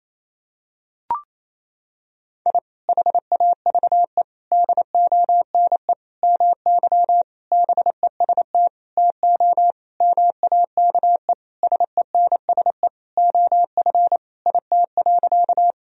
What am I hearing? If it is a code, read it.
Morse code, heard as a keyed tone, about 28 wpm, 700 Hz. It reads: I HAVE DONE MY BEST TO MAKE SENSE OF IT.
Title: Sentences from Top 1000 Words - Review Part 4 - 28wpm